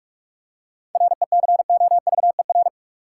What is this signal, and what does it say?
Morse code, heard as a keyed tone, about 48 wpm, 700 Hz. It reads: RECOVER
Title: Seven Letter Words - 45wpm